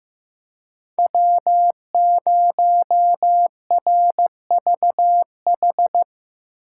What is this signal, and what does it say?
Morse code, heard as a keyed tone, about 15 wpm, 700 Hz. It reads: W0RVH
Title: USA Call Signs - 5 Characters - 3x Character Spacing - 15wpm